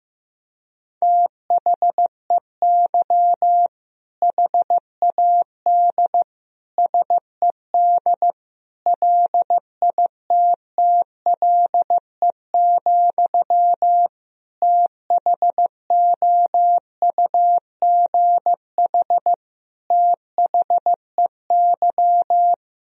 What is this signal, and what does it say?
Morse code, heard as a keyed tone, about 16 wpm, 700 Hz. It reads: THEY HAD SED LITTLE, THOUGH THEY
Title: Sentences from Top 400 Words and Abbreviations 15wpm